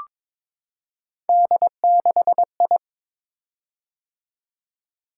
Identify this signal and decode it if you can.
Morse code, heard as a keyed tone, about 22 wpm, 700 Hz. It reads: D6I
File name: practice / 22wpm / Three Letter-Number Combinations - 22wpm